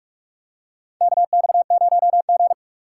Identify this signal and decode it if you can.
Morse code, heard as a keyed tone, about 46 wpm, 700 Hz. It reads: KX0G
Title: USA Call Signs - 4 Characters - 4x Character Spacing - 45wpm